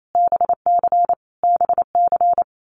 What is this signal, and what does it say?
Morse code, heard as a keyed tone, about 28 wpm, 700 Hz. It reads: BC BC
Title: Canadian Provinces and Territories - Repeated 3x - 28wpm